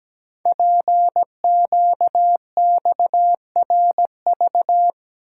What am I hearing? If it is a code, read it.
Morse code, heard as a keyed tone, about 17 wpm, 700 Hz. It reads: PQXRV
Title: Code Groups - 5 Letters - 17wpm